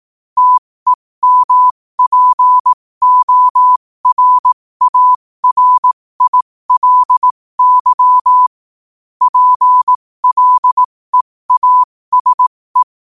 Morse code, heard as a keyed tone, about 18 wpm, spaced as for 15 wpm, 1000 Hz. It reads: TEMPORARILY PLEASE
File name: qotdfiles / 15wpm